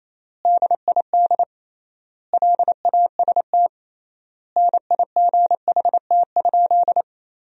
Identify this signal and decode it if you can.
Morse code, heard as a keyed tone, about 28 wpm, 700 Hz. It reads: DID LAST NIGHT?
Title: 3x Word-Spacing - Sentences from Top 300 Words - 28wpm